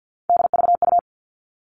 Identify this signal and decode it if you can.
Morse code, heard as a keyed tone, about 50 wpm, 700 Hz. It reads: DVU
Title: All Cut Numbers - 3-Digit 50wpm